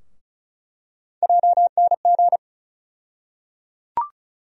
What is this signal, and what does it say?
Morse code, heard as a keyed tone, about 36 wpm, 700 Hz. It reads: JNG
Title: USA Call Sign Postfixes - 3 Letters 35wpm